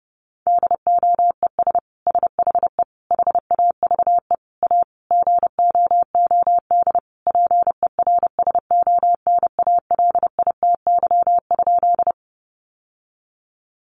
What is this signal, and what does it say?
Morse code, heard as a keyed tone, about 30 wpm, 700 Hz. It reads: DOES SHE HAVE A GOOD PERSONALITY?
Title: Sentences from Top 1400 Words 30wpm